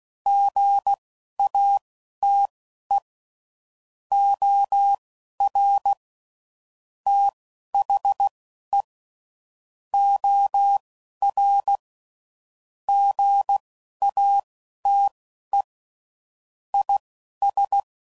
Morse code, heard as a keyed tone, about 16 wpm, spaced as for 11 wpm, 800 Hz. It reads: GATE OR THE OR GATE IS
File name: CW Code Groups @10WPM / Technical Subjects / AND Gate